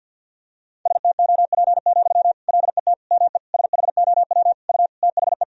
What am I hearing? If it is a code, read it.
Morse code, heard as a keyed tone, about 49 wpm, 700 Hz. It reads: STOP, LET ME SHOW U THE
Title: Sentences from Top 300 Words and Abbreviations 50wpm